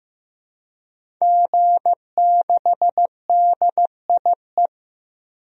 Morse code, heard as a keyed tone, about 15 wpm, 700 Hz. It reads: G6DIE
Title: DXCC-100 Call Signs - 5 Characters - 15wpm